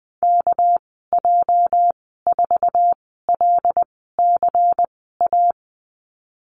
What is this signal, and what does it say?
Morse code, heard as a keyed tone, about 20 wpm, spaced as for 14 wpm, 700 Hz. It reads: KJ4LCA
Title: 213 Lesson 34g - Number 8 - Call Signs 20wpm